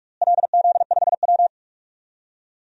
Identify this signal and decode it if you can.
Morse code, heard as a keyed tone, about 42 wpm, 700 Hz. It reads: LZ5W
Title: DXCC-100 Call Signs - 4 Characters - Repeated 3x - 45wpm